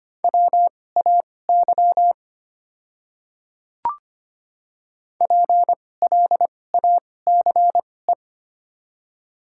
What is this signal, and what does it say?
Morse code, heard as a keyed tone, about 25 wpm, spaced as for 18 wpm, 700 Hz. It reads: WAY PLACE
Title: 2x Character Spacing - Top 500 Words - 25wpm